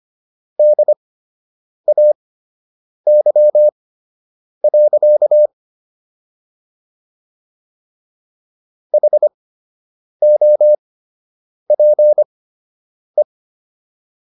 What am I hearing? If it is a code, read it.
Morse code, heard as a keyed tone, about 25 wpm, spaced as for 8 wpm, 600 Hz. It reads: DAY. HOPE